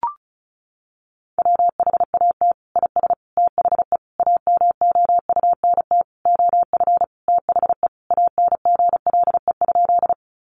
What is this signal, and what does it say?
Morse code, heard as a keyed tone, about 34 wpm, 700 Hz. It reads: WHAT IS THE AMOUNT OF THE ANGLE?
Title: Sentences from Top 1500 Words 35wpm